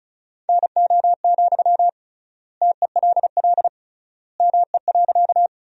Morse code, heard as a keyed tone, about 36 wpm, 700 Hz. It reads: NO, TELL ME.
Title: 3x Word-Spacing - Sentences from Top 200 Words - 35wpm